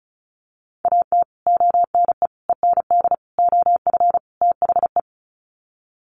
Morse code, heard as a keyed tone, about 34 wpm, 700 Hz. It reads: AT ONE END OF THE
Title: Sets of 5 Words Encore 35wpm